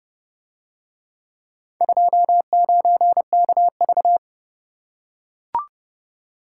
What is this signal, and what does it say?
Morse code, heard as a keyed tone, about 29 wpm, 700 Hz. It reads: 29KV